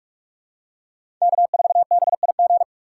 Morse code, heard as a keyed tone, about 42 wpm, 700 Hz. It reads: K4BIG
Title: USA Call Signs - 5 Characters - 8x Character Spacing - 45wpm